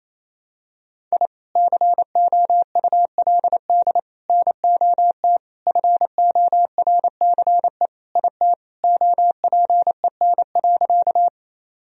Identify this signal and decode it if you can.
Morse code, heard as a keyed tone, about 27 wpm, 700 Hz. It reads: I COULD NOT FORCE IT OPEN.
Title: Sentences from Top 1000 Words - Review Part 3 - 28wpm